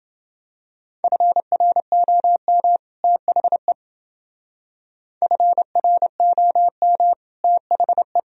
Morse code, heard as a keyed tone, about 29 wpm, 700 Hz. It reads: FROM THE FROM THE